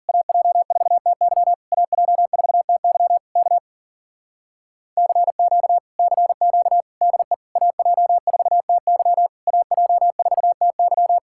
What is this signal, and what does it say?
Morse code, heard as a keyed tone, about 51 wpm, 700 Hz. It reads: AJ4TY AJ4TY K CQ CQ DE AJ4TY AJ4TY